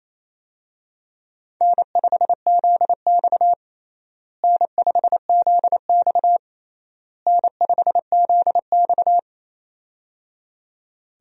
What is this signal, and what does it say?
Morse code, heard as a keyed tone, about 28 wpm, 700 Hz. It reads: N5ZX N5ZX N5ZX
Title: USA Call Signs - 4 Characters - Repeated 3x - 28wpm